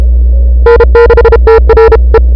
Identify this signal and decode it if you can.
Morse code, heard as a keyed tone, about 35 wpm, 430 Hz. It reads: N6TR E
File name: n6tr / Fab2007CWSprintHour1